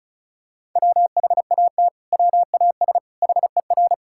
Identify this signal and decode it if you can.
Morse code, heard as a keyed tone, about 36 wpm, 700 Hz. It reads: WHAT WAS HER
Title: Sentences from Top 1000 Words - Review Part 5 - 35wpm